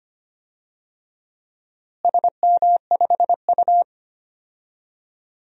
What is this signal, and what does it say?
Morse code, heard as a keyed tone, about 25 wpm, 700 Hz. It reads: SM5U
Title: DXCC-100 Call Signs - 4 Characters - Repeated 3x - 25wpm